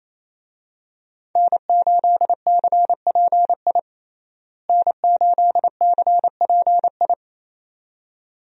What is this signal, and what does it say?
Morse code, heard as a keyed tone, about 28 wpm, 700 Hz. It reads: N8CPI N8CPI
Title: USA Call Signs - 5 Characters - Repeated 2x - 28wpm